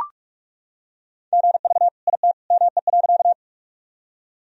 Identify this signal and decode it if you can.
Morse code, heard as a keyed tone, about 41 wpm, 700 Hz. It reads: GV IT ME.
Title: Sentences from Top 200 Words and Abbreviations 45wpm